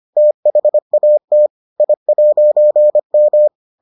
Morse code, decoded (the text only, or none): THAT I'M